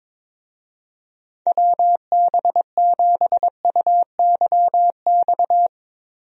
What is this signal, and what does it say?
Morse code, heard as a keyed tone, about 22 wpm, 700 Hz. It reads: WB7UYX
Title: USA Call Signs - 6 Characters - Repeated 3x - 22wpm